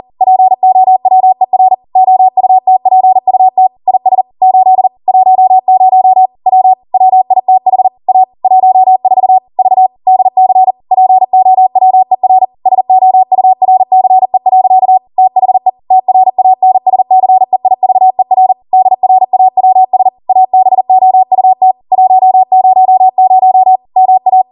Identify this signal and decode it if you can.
Morse code, heard as a keyed tone, about 40 wpm, 750 Hz. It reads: POWER OUTPUT IS 8 10 W WITH A 14 V DC POWER SOURCE. THE TRANSCEIVER DRAWS ABOUT 100 MA
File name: Morse / 40WPM